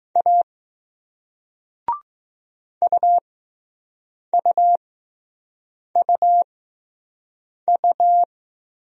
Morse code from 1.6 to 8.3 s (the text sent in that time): UUUU